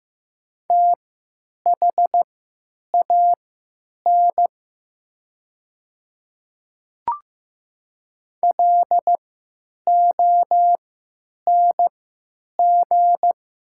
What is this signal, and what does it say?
Morse code, heard as a keyed tone, about 15 wpm, spaced as for 9 wpm, 700 Hz. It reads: THAN LONG
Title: 3x Character Spacing - Top 300 Words - 15wpm